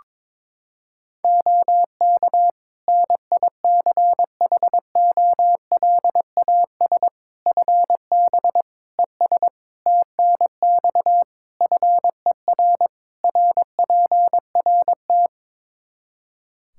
Morse code, heard as a keyed tone, about 22 wpm, 700 Hz. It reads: OK NICHOLAS FB ES TNX FER RPRT
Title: Practice QSO - Exchange 2 - Info - 22wpm